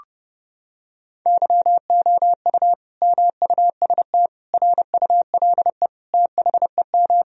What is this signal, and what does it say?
Morse code, heard as a keyed tone, about 30 wpm, 700 Hz. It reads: YOU MUST RULE THEM
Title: Sentences from Top 1000 Words - Review Part 2 - 30wpm